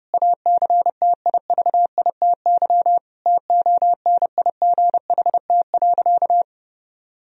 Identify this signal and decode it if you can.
Morse code, heard as a keyed tone, about 32 wpm, 700 Hz. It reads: ACTIVITY TONIGHT.
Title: Sentences from Top 1000 Words - Review Part 4 - 30wpm